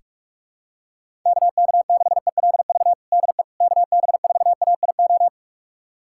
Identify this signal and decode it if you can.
Morse code, heard as a keyed tone, about 46 wpm, 700 Hz. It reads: KK6ELV DE KB4AIO
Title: USA Callsign DE Callsign - 6 Characters - 45wpm